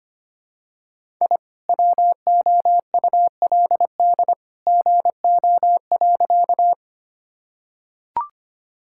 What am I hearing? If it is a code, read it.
Morse code, heard as a keyed tone, about 25 wpm, 700 Hz. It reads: I WOULD GO.
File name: practice / 25wpm / Sentences with 2-Word Phrases - 25wpm